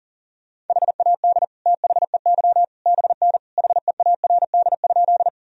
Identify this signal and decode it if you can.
Morse code, heard as a keyed tone, about 40 wpm, 700 Hz. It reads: HAD THEY BN HEARD?
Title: Sentences from Top 400 Words and Abbreviations 40wpm